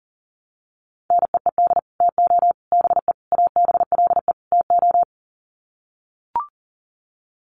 Morse code, heard as a keyed tone, about 40 wpm, 700 Hz. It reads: NEED TO BE ABLE TO
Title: Sets of 5 Words Encore 40wpm